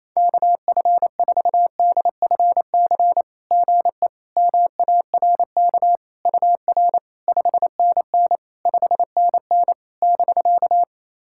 Morse code, heard as a keyed tone, about 28 wpm, 700 Hz. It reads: KF4DFC GE MARK UR 5NN 5NN <BK>
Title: SOTA QSO Practice 28wpm